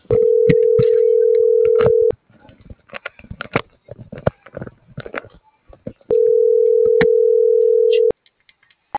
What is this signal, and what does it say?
Ringback tone (North America), 440 + 480 Hz.